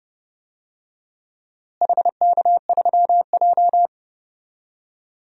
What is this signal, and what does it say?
Morse code, heard as a keyed tone, about 30 wpm, 700 Hz. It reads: HK3J
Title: DXCC-100 Call Signs - 4 Characters - Repeated 2x - 30wpm